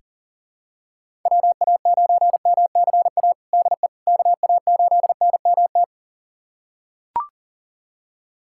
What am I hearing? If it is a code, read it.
Morse code, heard as a keyed tone, about 40 wpm, 700 Hz. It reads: WA9MCA DE KA8NMT
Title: USA Callsign DE Callsign - 6 Characters - 40wpm